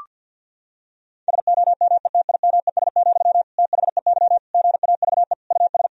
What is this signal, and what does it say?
Morse code, heard as a keyed tone, about 50 wpm, 700 Hz. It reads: SOMETIMES, THEY GAVE US
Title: Sentences from Top 1000 Words - Review Part 4 - 50wpm